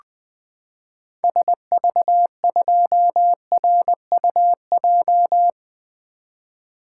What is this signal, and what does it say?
Morse code, heard as a keyed tone, about 20 wpm, 700 Hz. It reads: SV2RUJ
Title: 2023 SOTA Activator Call Signs - 20wpm